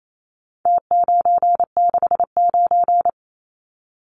Morse code, heard as a keyed tone, about 28 wpm, 700 Hz. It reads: T969
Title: Numbers - Contest Style - 4-Digit 28wpm